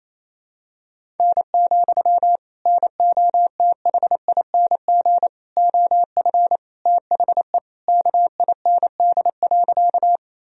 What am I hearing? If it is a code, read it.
Morse code, heard as a keyed tone, about 28 wpm, 700 Hz. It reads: N, NOTHING OF THE KIND.
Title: Sentences from Top 500 Words and Abbreviations 28wpm